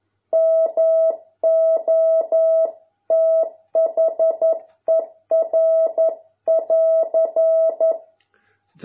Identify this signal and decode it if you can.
Morse code, heard as a keyed tone, about 11 wpm, 650 Hz. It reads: MOTHER<AR>